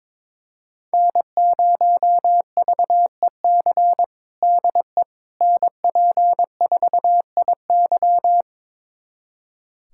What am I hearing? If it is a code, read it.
Morse code, heard as a keyed tone, about 22 wpm, 700 Hz. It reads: N0VEC DE NP4IY
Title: USA Callsign DE Callsign - 5 Characters - 22wpm